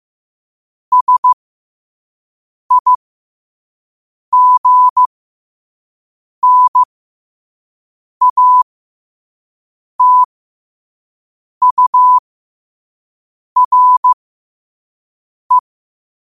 Morse code, heard as a keyed tone, about 15 wpm, spaced as for 5 wpm, 1000 Hz. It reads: SIGNATURE
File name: qotdfiles / 5wpm